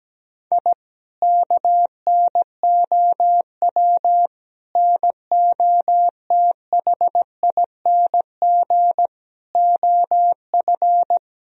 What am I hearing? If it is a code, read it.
Morse code, heard as a keyed tone, about 17 wpm, 700 Hz. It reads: I KNOW NOTHING OF
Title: Sentences from Top 1000 Words - Review Part 3 - 17wpm